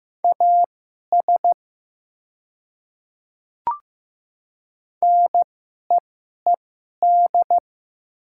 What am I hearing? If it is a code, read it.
Morse code, heard as a keyed tone, about 15 wpm, spaced as for 11 wpm, 700 Hz. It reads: AS NEED
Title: 2x Character Spacing - Top 200 Words - 15wpm